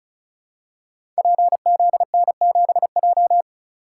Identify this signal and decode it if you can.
Morse code, heard as a keyed tone, about 35 wpm, 700 Hz. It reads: PZN7J
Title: Code Groups - 5 Letter:Numbers - 35wpm